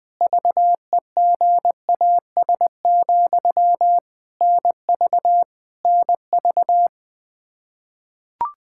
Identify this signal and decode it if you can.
Morse code, heard as a keyed tone, about 20 wpm, 700 Hz. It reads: VEGAS, NV NV
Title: K1USN SST QSO Practice 20wpm